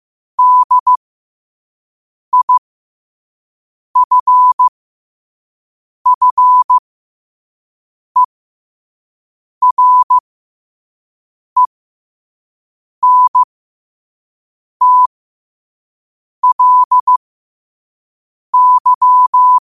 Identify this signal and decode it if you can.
Morse code, heard as a keyed tone, about 15 wpm, spaced as for 5 wpm, 1000 Hz. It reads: DIFFERENTLY
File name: qotdfiles / 5wpm